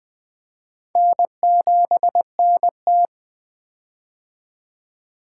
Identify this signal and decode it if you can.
Morse code, heard as a keyed tone, about 21 wpm, 700 Hz. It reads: N7NT
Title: K1USN SST QSO Practice - 20wpm